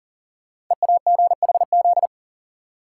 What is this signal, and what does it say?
Morse code, heard as a keyed tone, about 42 wpm, 690 Hz. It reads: EAGHZ
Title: Code Groups - 5 Letters - 40wpm